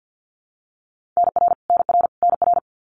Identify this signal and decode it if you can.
Morse code, heard as a keyed tone, about 48 wpm, 700 Hz. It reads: NR NR NR
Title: Abbreviations Repeated 3x - 50wpm